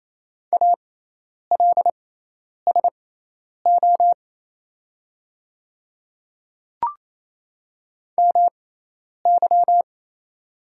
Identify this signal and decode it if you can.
Morse code, heard as a keyed tone, about 28 wpm, spaced as for 10 wpm, 700 Hz. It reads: ALSO MY
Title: 6x Character Spacing - Top 500 Words - 28wpm